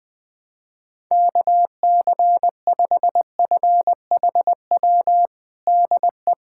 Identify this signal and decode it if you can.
Morse code, heard as a keyed tone, about 21 wpm, 700 Hz. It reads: KC5FHW DE
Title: USA Callsign DE Callsign 20wpm